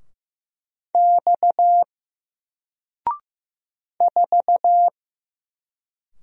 Morse code, heard as a keyed tone, about 15 wpm, 700 Hz. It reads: X 4
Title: Single Letter-Number Alt 15wpm